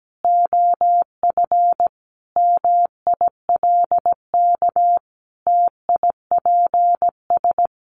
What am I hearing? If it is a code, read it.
Morse code, heard as a keyed tone, about 17 wpm, 700 Hz. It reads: OF MILK TIPS
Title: Additional Easy Phrases 17wpm